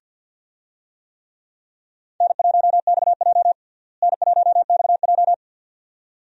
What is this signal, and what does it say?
Morse code, heard as a keyed tone, about 53 wpm, 700 Hz. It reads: N1XJ N1XJ
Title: USA Call Signs - 4 Characters - Repeated 2x - 50wpm